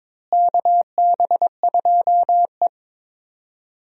Morse code, heard as a keyed tone, about 21 wpm, 700 Hz. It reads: KB2E
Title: USA Call Signs - 4 Characters - Repeated 3x - 22wpm